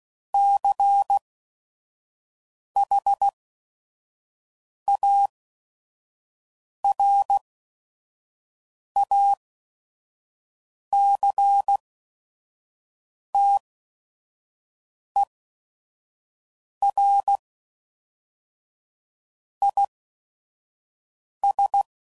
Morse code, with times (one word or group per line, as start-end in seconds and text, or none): 0.3-17.4 CHARACTER
19.6-21.8 IS